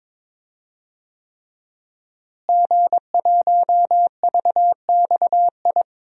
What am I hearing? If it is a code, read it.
Morse code, heard as a keyed tone, about 23 wpm, 700 Hz. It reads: G1VXI
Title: DXCC-100 Call Signs - 5 Characters - Repeated 3x - 22wpm